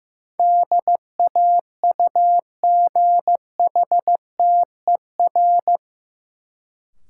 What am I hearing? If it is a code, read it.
Morse code, heard as a keyed tone, about 15 wpm, 700 Hz. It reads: DAUGHTER
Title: Sentences from Top 1500 Words - New Word Drill - 15wpm